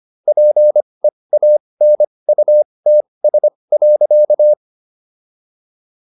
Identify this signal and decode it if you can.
Morse code, heard as a keyed tone, about 25 wpm, spaced as for 20 wpm, 600 Hz. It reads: PEANUTS.